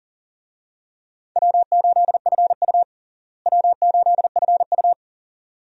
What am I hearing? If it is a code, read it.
Morse code, heard as a keyed tone, about 40 wpm, 700 Hz. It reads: W8FU W8FU
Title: USA Call Signs - 4 Characters - Repeated 2x - 40wpm